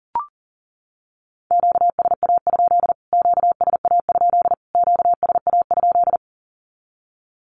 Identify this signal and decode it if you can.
Morse code, heard as a keyed tone, about 40 wpm, 700 Hz. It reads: QSA? QSA? QSA?
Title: Q-Codes Repeated 3x 40wpm